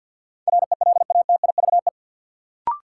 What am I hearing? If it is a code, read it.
Morse code, heard as a keyed tone, about 54 wpm, 700 Hz. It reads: RELATIVE
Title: Sentences from Top 1400 Words - New Word Drill - 50wpm